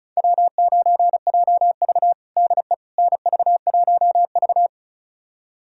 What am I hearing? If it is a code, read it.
Morse code, heard as a keyed tone, about 34 wpm, 700 Hz. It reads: W9JV DE NV1V
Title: USA Callsign DE Callsign - 4 Characters - 35wpm